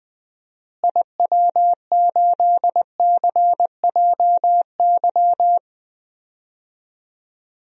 Morse code, heard as a keyed tone, about 20 wpm, 700 Hz. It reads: IW8CJY
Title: DXCC-100 Call Signs - 6 Characters - Repeated 2x - 20wpm